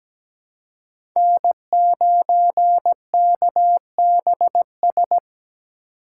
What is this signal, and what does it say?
Morse code, heard as a keyed tone, about 17 wpm, 700 Hz. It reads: N9KBS
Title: USA Call Signs - 5 Characters - 2x Character Spacing - 17wpm